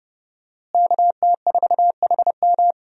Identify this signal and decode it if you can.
Morse code, heard as a keyed tone, about 30 wpm, 700 Hz. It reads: KT4HM
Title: USA Call Signs - 5 Characters - Repeated 2x - 30wpm